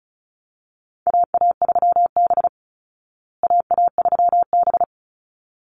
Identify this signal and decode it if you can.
Morse code, heard as a keyed tone, about 34 wpm, 700 Hz. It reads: AA3B AA3B
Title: Call Signs Repeated 3x - 35wpm